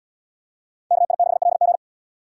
Morse code, heard as a keyed tone, about 77 wpm, 700 Hz. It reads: KI6LG